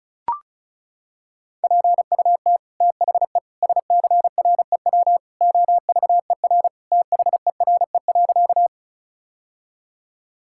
Morse code, heard as a keyed tone, about 36 wpm, 700 Hz. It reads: PUT THE SCREW OVER THERE.
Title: Sentences from Top 1600 Words 35wpm